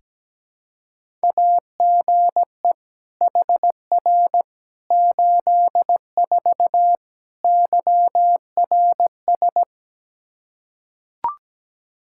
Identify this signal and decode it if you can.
Morse code, heard as a keyed tone, about 17 wpm, 700 Hz. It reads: AGE HR 84 YRS
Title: Practice QSO - Exchange 3 - More Info - 17wpm